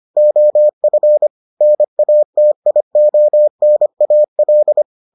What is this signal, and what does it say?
Morse code, heard as a keyed tone, about 25 wpm, 600 Hz. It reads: OF NATIONAL